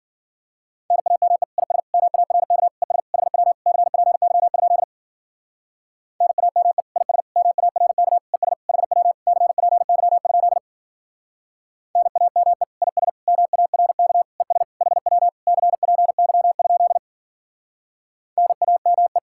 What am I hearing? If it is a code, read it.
Morse code, heard as a keyed tone, about 53 wpm, 700 Hz. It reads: NAME IS MARK ES HW CPY? NAME IS MARK ES HW CPY? NAME IS MARK ES HW CPY? NAME